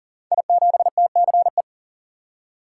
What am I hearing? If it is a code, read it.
Morse code, heard as a keyed tone, about 39 wpm, 700 Hz. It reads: I7TCE
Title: Code Groups - 5 Letter:Numbers - 40wpm